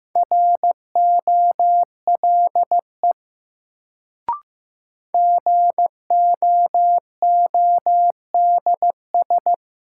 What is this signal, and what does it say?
Morse code, heard as a keyed tone, about 15 wpm, 700 Hz. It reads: ROLE GOODS
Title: Sentences from Top 1000 Words - Review Part 6 - New Word Drill - 15wpm